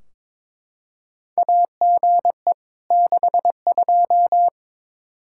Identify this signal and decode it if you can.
Morse code, heard as a keyed tone, about 21 wpm, 700 Hz. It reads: AGE 62
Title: Practice QSO - Exchange 3 - More Info - 22wpm